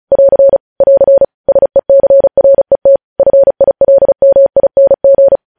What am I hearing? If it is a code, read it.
Morse code, heard as a keyed tone, about 36 wpm, 550 Hz. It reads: <AR> <AR> SECRET FILMING
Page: News headlines in Morse code at 35 wpm